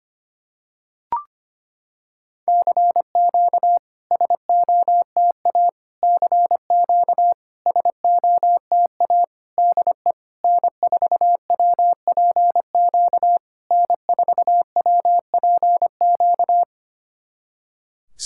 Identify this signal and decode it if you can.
Morse code, heard as a keyed tone, about 25 wpm, 690 Hz. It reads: CQ SOTA CQ SOTA DE N4WPQ N4WPQ
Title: SOTA QSO Practice 25wpm